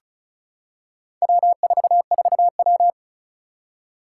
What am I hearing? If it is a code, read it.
Morse code, heard as a keyed tone, about 35 wpm, 700 Hz. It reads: W44W
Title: Four Challenging Letter-Number Combinations 35wpm